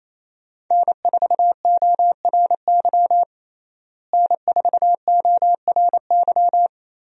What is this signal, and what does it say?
Morse code, heard as a keyed tone, about 27 wpm, 700 Hz. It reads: N4ORY N4ORY
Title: USA Call Signs - 5 Characters - Repeated 2x - 28wpm